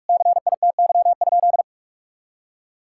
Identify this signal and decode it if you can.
Morse code, heard as a keyed tone, about 41 wpm, 700 Hz. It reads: CITY?
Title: Sentences from Top 1000 Words - Review Part 5 - 45wpm